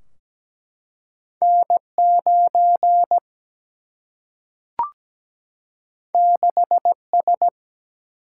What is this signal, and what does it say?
Morse code, heard as a keyed tone, about 17 wpm, 700 Hz. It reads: N 9 6 S
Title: Single-Letter Single-Number 17wpm